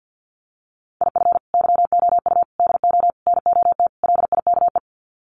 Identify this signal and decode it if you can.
Morse code, heard as a keyed tone, about 50 wpm, 700 Hz. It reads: IF YOU DO NOT LIKE
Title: Sets of 5 Words 50wpm